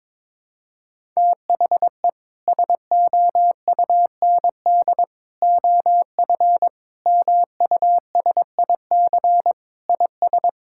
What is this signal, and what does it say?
Morse code, heard as a keyed tone, about 22 wpm, 700 Hz. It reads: THE SOUND OF MUSIC IS
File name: practice / 22wpm / Sentences with 3-Word Phrases - 22wpm